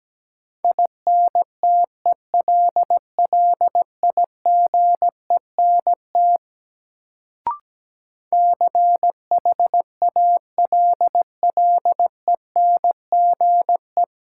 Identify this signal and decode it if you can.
Morse code, heard as a keyed tone, about 17 wpm, 700 Hz. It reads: INTELLIGENT CHALLENGE
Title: Sentences from Top 1000 Words - Review Part 7 - New Word Drill - 17wpm